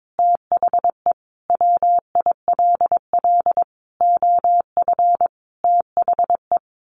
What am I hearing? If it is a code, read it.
Morse code, heard as a keyed tone, about 22 wpm, 700 Hz. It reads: THE WILL OF THE
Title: Sentences from Top 1500 Words 22wpm